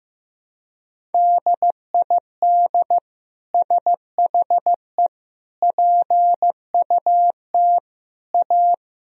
Morse code, heard as a keyed tone, about 15 wpm, 700 Hz. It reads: DID SHE PUT A
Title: Sentences from Top 1000 Words - Review Part 1 - 15wpm